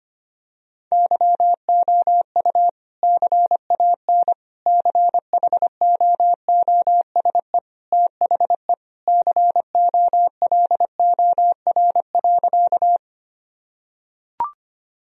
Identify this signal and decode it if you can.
Morse code, heard as a keyed tone, about 25 wpm, 700 Hz. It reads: YOU CAN CHOOSE THE COLOR.
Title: Sentences from Top 1000 Words - Review Part 4 - 25wpm